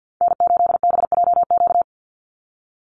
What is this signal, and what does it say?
Morse code, heard as a keyed tone, about 50 wpm, 700 Hz. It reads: N8BJQ
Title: Call Signs Repeated 3x - 50wpm